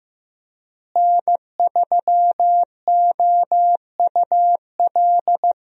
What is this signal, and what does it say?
Morse code, heard as a keyed tone, about 16 wpm, 700 Hz. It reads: N3OUL
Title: USA Call Signs - 5 Characters - Repeated 3x - 15wpm